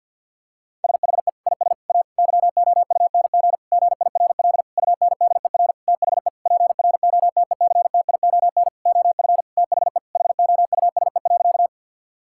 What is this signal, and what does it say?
Morse code, heard as a keyed tone, about 50 wpm, 700 Hz. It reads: SHE IS A YOUNG GIRL UNDER THE PROTECTION OF THE HOUSE.
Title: Sentences from Top 1600 Words 50wpm